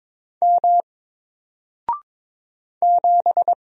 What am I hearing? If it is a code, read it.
Morse code, heard as a keyed tone, about 22 wpm, 700 Hz. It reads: M 7
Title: Single Letter-Number Alt 22wpm